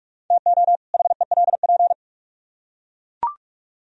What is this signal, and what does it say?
Morse code, heard as a keyed tone, about 40 wpm, 700 Hz. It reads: TO HELP